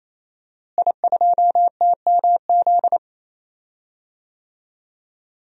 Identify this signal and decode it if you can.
Morse code, heard as a keyed tone, about 28 wpm, 690 Hz. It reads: I2TMZ
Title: DXCC-100 Call Signs - 5 Characters - Repeated 3x - 28wpm